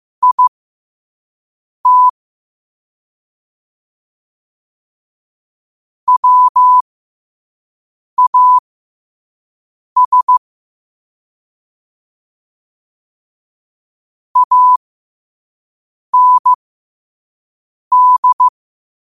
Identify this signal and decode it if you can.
Morse code, heard as a keyed tone, about 15 wpm, spaced as for 5 wpm, 1000 Hz. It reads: IT WAS AND
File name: qotdfiles / 5wpm